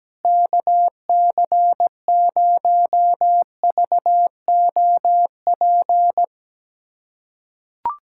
Morse code, heard as a keyed tone, about 17 wpm, 700 Hz. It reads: KC0VOP
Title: USA Callsign DE Callsign 17wpm